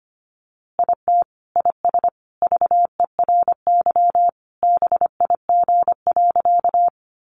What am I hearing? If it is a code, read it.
Morse code, heard as a keyed tone, about 25 wpm, 700 Hz. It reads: IT IS VERY BIG.
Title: Easy Phrases 25wpm